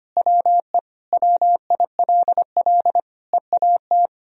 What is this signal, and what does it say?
Morse code, heard as a keyed tone, about 25 wpm, 700 Hz. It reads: WE WILL EAT
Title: Sentences from Top 1000 Words - Review Part 4 - 25wpm